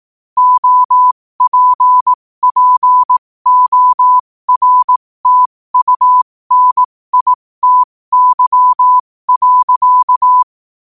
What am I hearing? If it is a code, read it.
Morse code, heard as a keyed tone, about 18 wpm, spaced as for 15 wpm, 1000 Hz. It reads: OPPORTUNITY.
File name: qotdfiles / 15wpm